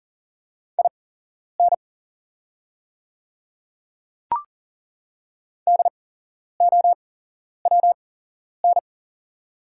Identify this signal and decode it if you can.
Morse code, heard as a keyed tone, about 40 wpm, spaced as for 11 wpm, 700 Hz. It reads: IN DOWN
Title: 8x Character Spacing - Top 200 Words - 40wpm